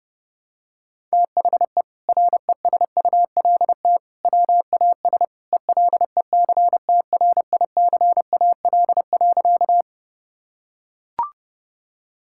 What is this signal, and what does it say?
Morse code, heard as a keyed tone, about 32 wpm, 700 Hz. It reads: THE RESULT WAS ELECTRICAL.
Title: Sentences from Top 1000 Words - Review Part 6 - 30wpm